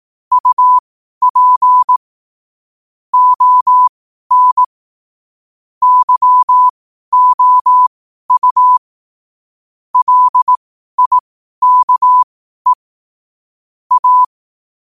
Morse code, heard as a keyed tone, about 18 wpm, spaced as for 12 wpm, 1000 Hz. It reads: UP ON YOU LIKE A
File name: qotdfiles / 12wpm